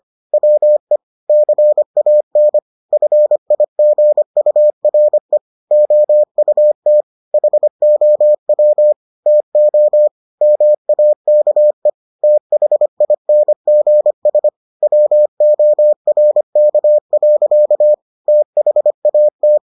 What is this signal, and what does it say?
Morse code, heard as a keyed tone, about 25 wpm, 600 Hz. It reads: WE CAN FIGURE OUT HOW TO MAKE THINGS WORK. THAT